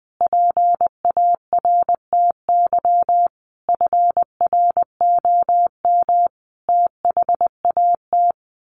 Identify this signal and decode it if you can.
Morse code, heard as a keyed tone, about 19 wpm, 700 Hz. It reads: PARTY FROM THAT
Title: Sentences from Top 1400 Words 20wpm